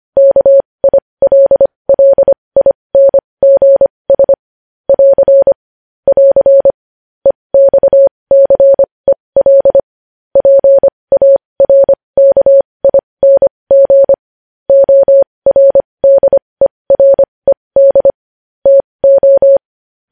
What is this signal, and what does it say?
Morse code, heard as a keyed tone, about 25 wpm, spaced as for 20 wpm, 550 Hz. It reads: KILLINGS <AR> <AR> EXCEL PARKING ORDERED TO